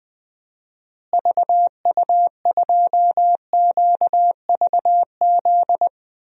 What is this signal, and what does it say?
Morse code, heard as a keyed tone, about 20 wpm, 700 Hz. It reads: VU2QVZ
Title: DXCC-100 Call Signs - 6 Characters - Repeated 3x - 20wpm